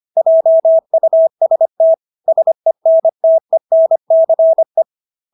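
Morse code, heard as a keyed tone, about 25 wpm, 650 Hz. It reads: JUST SENTENCE
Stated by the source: Morse practice of English words at 25 wpm from CW med Gnister.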